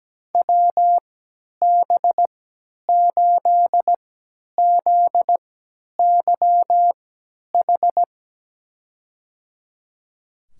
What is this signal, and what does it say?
Morse code, heard as a keyed tone, about 17 wpm, spaced as for 10 wpm, 700 Hz. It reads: WB8ZYH
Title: USA Call Signs - 6 Characters - 3x Character Spacing - 17wpm